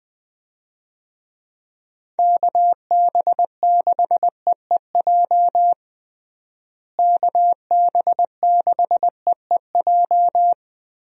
Morse code, heard as a keyed tone, about 21 wpm, 700 Hz. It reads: KB6EEJ KB6EEJ
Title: USA Call Signs - 6 Characters - Repeated 2x - 20wpm